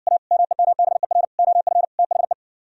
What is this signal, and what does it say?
Morse code, heard as a keyed tone, about 69 wpm, 700 Hz. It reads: A MEMBER OF THE